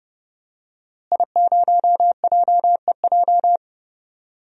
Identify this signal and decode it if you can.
Morse code, heard as a keyed tone, about 30 wpm, 700 Hz. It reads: I0JEJ
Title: DXCC-100 Call Signs - 5 Characters - Repeated 2x - 30wpm